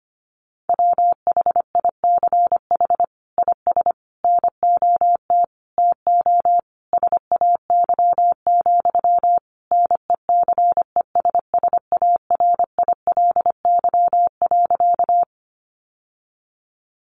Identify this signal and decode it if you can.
Morse code, heard as a keyed tone, about 25 wpm, 700 Hz. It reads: WHICH IS NOT TO SAY, NECESSARILY.
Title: Sentences from Top 1400 Words 25wpm